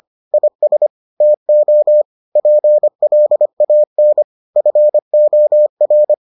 Morse code, heard as a keyed tone, about 25 wpm, 600 Hz. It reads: IS TO PLAN FOR